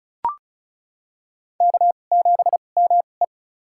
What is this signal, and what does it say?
Morse code, heard as a keyed tone, about 35 wpm, spaced as for 25 wpm, 700 Hz. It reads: K7ME
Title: USA Call Signs - 4 Characters - 2x Character Spacing - 35wpm